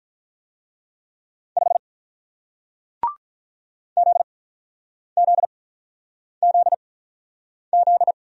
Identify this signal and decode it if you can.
Morse code, heard as a keyed tone, about 48 wpm, spaced as for 9 wpm, 700 Hz. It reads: 5 ZZZZ